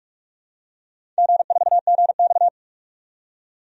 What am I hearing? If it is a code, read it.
Morse code, heard as a keyed tone, about 46 wpm, 700 Hz. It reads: G4GX
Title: DXCC-100 Call Signs - 4 Characters - 45wpm